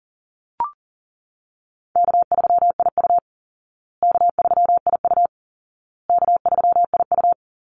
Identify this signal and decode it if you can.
Morse code, heard as a keyed tone, about 40 wpm, 700 Hz. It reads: K3IU K3IU K3IU
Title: Easy Call Signs Repeated 3x - 40wpm